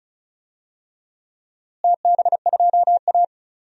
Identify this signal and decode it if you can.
Morse code, heard as a keyed tone, about 36 wpm, 700 Hz. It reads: TB2A
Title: DXCC-100 Call Signs - 4 Characters - Repeated 3x - 35wpm